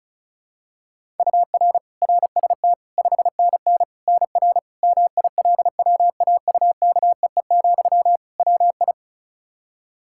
Morse code, heard as a keyed tone, about 36 wpm, 700 Hz. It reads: UR RST 5NN NR MILWAUKEE, WI
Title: Practice QSO - Exchange 1 - Beginning - 35wpm